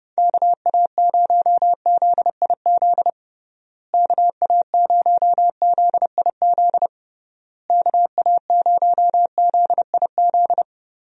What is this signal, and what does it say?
Morse code, heard as a keyed tone, about 30 wpm, 700 Hz. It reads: KA0ZIZ KA0ZIZ KA0ZIZ
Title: USA Call Signs - 6 Characters - Repeated 3x - 30wpm